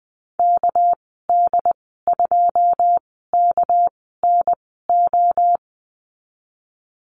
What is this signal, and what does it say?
Morse code, heard as a keyed tone, about 20 wpm, spaced as for 14 wpm, 700 Hz. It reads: KD2KNO
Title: 213 Lesson 34g - Number 8 - Call Signs 20wpm